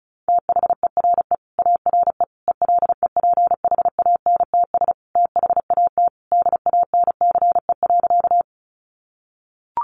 Morse code, heard as a keyed tone, about 34 wpm, 700 Hz. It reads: THERE ARE ELEPHANTS THAT DANCE.
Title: Easy Phrases 35wpm